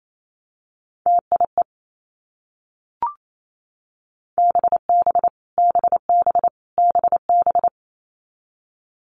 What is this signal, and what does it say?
Morse code, heard as a keyed tone, about 28 wpm, 700 Hz. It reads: TIE BB BB BB
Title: Multi-letter Phonemes Repeated 3x 28wpm